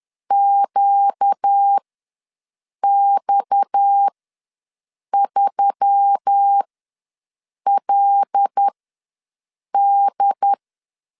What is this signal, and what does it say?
Morse code, heard as a keyed tone, about 10 wpm, spaced as for 6 wpm, 800 Hz. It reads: QX3LD
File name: LEZIONI / 6 PKX0 / 6 PKX ws 10wpm 3706